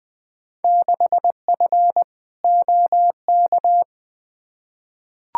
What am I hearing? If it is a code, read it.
Morse code, heard as a keyed tone, about 20 wpm, 700 Hz. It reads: 6F OK
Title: Field Day Exchanges Encore - 20wpm